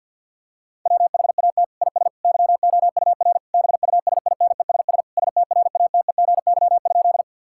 Morse code, heard as a keyed tone, about 51 wpm, 690 Hz. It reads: WHAT IS YOUR BUSINESS STRATEGY?
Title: Sentences from Top 1000 Words - Review Part 5 - 50wpm